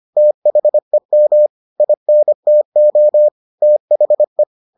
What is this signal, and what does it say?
Morse code, heard as a keyed tone, about 25 wpm, 600 Hz. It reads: THEM INTO THE